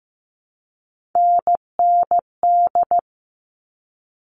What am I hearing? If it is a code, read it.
Morse code, heard as a keyed tone, about 15 wpm, 700 Hz. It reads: NND
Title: All Cut Numbers - 3-Digit 15wpm